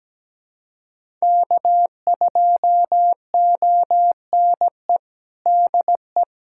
Morse code, heard as a keyed tone, about 17 wpm, 700 Hz. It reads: K2ONE DE
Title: USA Callsign DE Callsign - 5 Characters - 17wpm